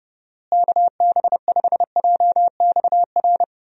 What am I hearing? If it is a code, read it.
Morse code, heard as a keyed tone, about 30 wpm, 700 Hz. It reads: KB5JXR
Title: USA Call Signs - 6 Characters - Repeated 3x - 30wpm